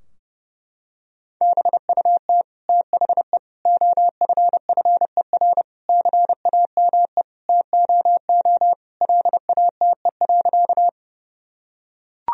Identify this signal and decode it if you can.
Morse code, heard as a keyed tone, about 30 wpm, 700 Hz. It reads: BUT THE OFFER CAME TOO LATE.
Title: Sentences from Top 1000 Words - Review Part 4 - 30wpm